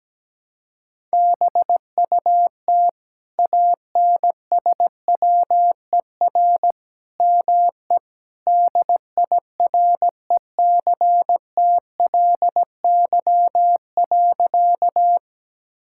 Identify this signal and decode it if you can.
Morse code, heard as a keyed tone, about 17 wpm, 700 Hz. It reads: BUT ANSWER ME DIRECTLY.
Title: Sentences from Top 1000 Words - Review Part 5 - 17wpm